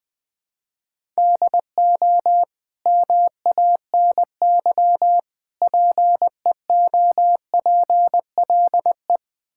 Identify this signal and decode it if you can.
Morse code, heard as a keyed tone, about 20 wpm, 700 Hz. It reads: DO MANY PEOPLE
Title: Sentences from Top 1000 Words - Review Part 6 - 20wpm